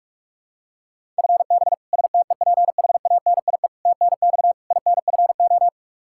Morse code, heard as a keyed tone, about 46 wpm, 700 Hz. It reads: FB STEPHANIE TNX INFO
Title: Practice QSO - Exchange 3 - More Info - 45wpm